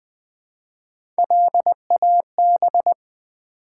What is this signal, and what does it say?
Morse code, heard as a keyed tone, about 20 wpm, 700 Hz. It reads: LAB
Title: Build A Word - 1 Letter At A Time - 3-Letter Words - 20wpm